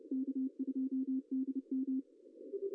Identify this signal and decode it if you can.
Morse code, heard as a keyed tone, about 30 wpm, 250 Hz. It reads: K2DM
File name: CW / cqww-cw-k5tr